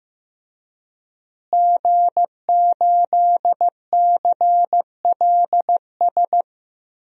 Morse code, heard as a keyed tone, about 15 wpm, 700 Hz. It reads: G8CLS
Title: DXCC-100 Call Signs - 5 Characters - Repeated 3x - 15wpm